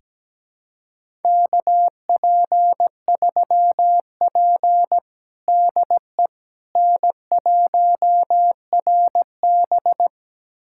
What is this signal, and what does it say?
Morse code, heard as a keyed tone, about 17 wpm, 700 Hz. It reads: KP3P DE N1RB
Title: USA Callsign DE Callsign - 4 Characters - 17wpm